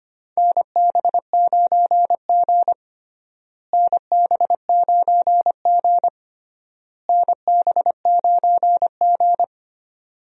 Morse code, heard as a keyed tone, about 25 wpm, 700 Hz. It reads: NB9G NB9G NB9G
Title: USA Call Signs - 4 Characters - Repeated 3x - 25wpm